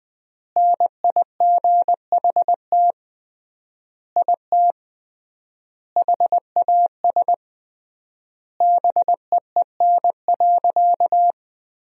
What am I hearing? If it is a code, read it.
Morse code, heard as a keyed tone, about 20 wpm, 700 Hz. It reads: NIGHT IT HAS BEEN.
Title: 3x Word-Spacing - Sentences from Top 300 Words - 20wpm